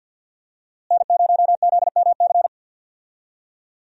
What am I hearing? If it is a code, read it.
Morse code, heard as a keyed tone, about 49 wpm, 700 Hz. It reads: N0ZMC
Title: USA Call Signs - 5 Characters - Repeated 3x - 50wpm